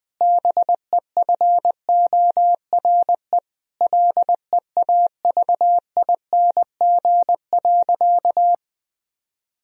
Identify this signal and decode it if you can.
Morse code, heard as a keyed tone, about 20 wpm, 700 Hz. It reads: BEFORE LEAVING.
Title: Sentences from Top 1000 Words - Review Part 4 - 20wpm